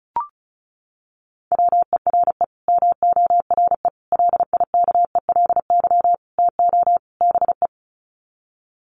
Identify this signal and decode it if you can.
Morse code, heard as a keyed tone, about 34 wpm, 700 Hz. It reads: WERE MORE LIKELY TO BE
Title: Sets of 5 Words Encore2 35wpm